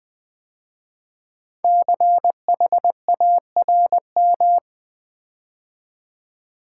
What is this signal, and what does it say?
Morse code, heard as a keyed tone, about 20 wpm, 700 Hz. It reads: CHARM
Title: 5 Letter Words - First 3 Letters Given - 20wpm